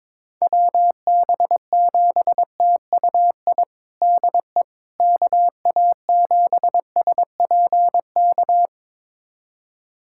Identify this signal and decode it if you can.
Morse code, heard as a keyed tone, about 22 wpm, 700 Hz. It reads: WB7TUI DE KA7SPK
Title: USA Callsign DE Callsign 22wpm